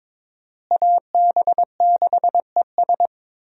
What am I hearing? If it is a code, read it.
Morse code, heard as a keyed tone, about 21 wpm, 700 Hz. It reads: AB6ES
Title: USA Call Signs - 5 Characters - Repeated 2x - 22wpm